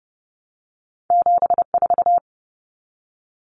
Morse code, heard as a keyed tone, about 30 wpm, 700 Hz. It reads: 74
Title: Common Cut Numbers - 2-Digit 30wpm